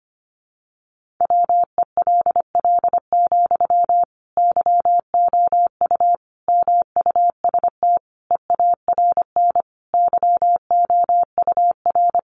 Morse code, heard as a keyed tone, about 25 wpm, 700 Hz. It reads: WELL, YOU MUST EARN YOUR
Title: Sentences from Top 1500 Words 25wpm